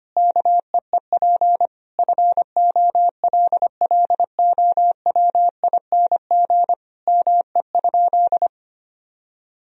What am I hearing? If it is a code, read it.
Morse code, heard as a keyed tone, about 25 wpm, 700 Hz. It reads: KEEP FOLLOWING ME?
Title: Sentences from Top 1000 Words - Review Part 4 - 25wpm